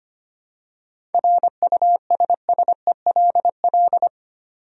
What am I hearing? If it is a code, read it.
Morse code, heard as a keyed tone, about 25 wpm, 700 Hz. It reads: RUSSELL
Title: USA Common Names - 25wpm